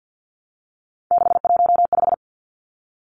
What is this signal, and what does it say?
Morse code, heard as a keyed tone, about 50 wpm, 700 Hz. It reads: 615
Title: Numbers - 3-Digit 50wpm